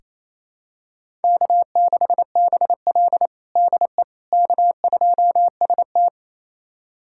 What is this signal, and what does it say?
Morse code, heard as a keyed tone, about 28 wpm, 700 Hz. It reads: K6BL DE K2ST
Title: USA Callsign DE Callsign - 4 Characters - 28wpm